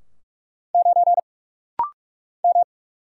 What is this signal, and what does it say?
Morse code, heard as a keyed tone, about 44 wpm, 700 Hz. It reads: 9 M
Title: Single Letter-Number Alt - Rapid-Fire 45wpm